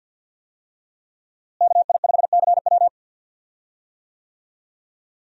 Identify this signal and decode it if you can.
Morse code, heard as a keyed tone, about 51 wpm, 700 Hz. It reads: KI5CW
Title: USA Call Signs - 5 Characters - Repeated 2x - 50wpm